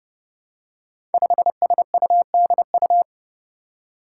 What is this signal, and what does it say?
Morse code, heard as a keyed tone, about 30 wpm, 700 Hz. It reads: 5SUDU
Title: Five Challenging Letter-Number Combinations 30wpm